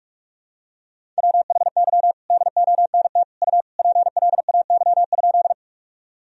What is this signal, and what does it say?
Morse code, heard as a keyed tone, about 46 wpm, 700 Hz. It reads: WHY DONT U PLAY?
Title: Sentences from Top 300 Words and Abbreviations 45wpm